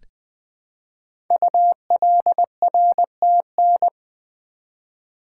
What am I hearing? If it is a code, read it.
Morse code, heard as a keyed tone, about 20 wpm, 700 Hz. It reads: ULRTN
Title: Code Groups - 5 Letters - 20wpm